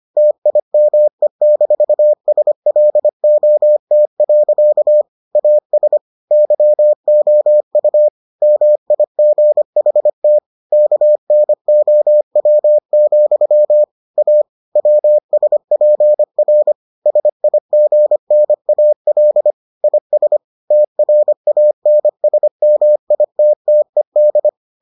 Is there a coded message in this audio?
Yes: Morse code from 0.2 to 24.5 s, heard as a keyed tone, about 25 wpm, 600 Hz. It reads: TIME-SLOT. AS YOU MIGHT KNOW, A WSPR SIGNAL IS TRANSMITTED